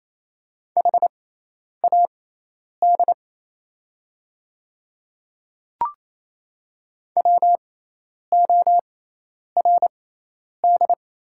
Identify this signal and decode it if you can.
Morse code, heard as a keyed tone, about 28 wpm, spaced as for 10 wpm, 700 Hz. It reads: HAD WORD
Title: 6x Character Spacing - Top 100 Words - 28wpm